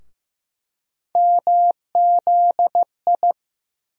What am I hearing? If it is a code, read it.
Morse code, heard as a keyed tone, about 15 wpm, 700 Hz. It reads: MZI
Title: USA Call Sign Postfixes - 3 Letters 15wpm